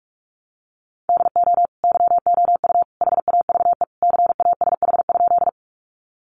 Morse code, heard as a keyed tone, about 44 wpm, 700 Hz. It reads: DO YOU HAVE CASH?
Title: Sentences from Top 700 Words 45wpm